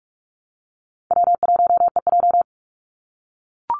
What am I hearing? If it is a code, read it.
Morse code, heard as a keyed tone, about 44 wpm, 700 Hz. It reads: W1EJ
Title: NA SOTA Activator Callsigns 45wpm